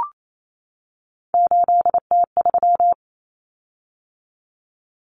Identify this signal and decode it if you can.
Morse code, heard as a keyed tone, about 28 wpm, 700 Hz. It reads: 8T3
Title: Common Cut Numbers - 3-Digit 28wpm